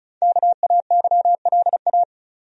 Morse code, heard as a keyed tone, about 36 wpm, 700 Hz. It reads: KAYLA
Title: USA Common Names - 35wpm